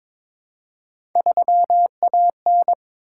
Morse code, heard as a keyed tone, about 22 wpm, 700 Hz. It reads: 3AN
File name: practice / 22wpm / Three Letter-Number Combinations - 22wpm